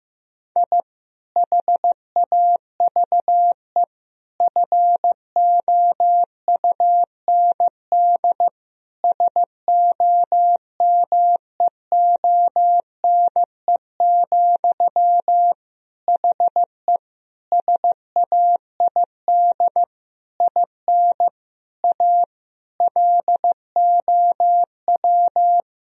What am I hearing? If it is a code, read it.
Morse code, heard as a keyed tone, about 15 wpm, 700 Hz. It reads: I HAVE FOUND SOMEONE, HE SAID IN A LOW
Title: Sentences from Top 1000 Words - Review Part 1 - 15wpm